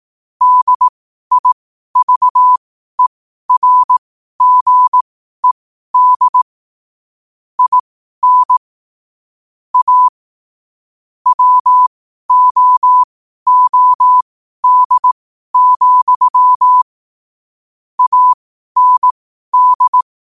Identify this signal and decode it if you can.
Morse code, heard as a keyed tone, about 18 wpm, spaced as for 12 wpm, 1000 Hz. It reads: DIVERGED IN A WOOD, AND
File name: qotdfiles / 12wpm